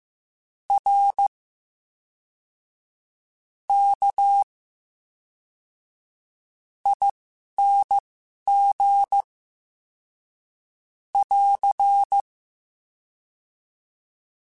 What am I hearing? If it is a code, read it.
Morse code, heard as a keyed tone, about 15 wpm, spaced as for 5 wpm, 800 Hz. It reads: RKING <AR>
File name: Special INGs / Special INGs-mp3s / VerySlowToFaster Ing @10